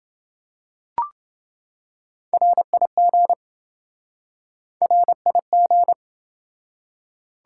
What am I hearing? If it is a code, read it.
Morse code, heard as a keyed tone, about 26 wpm, 700 Hz. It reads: RIG RIG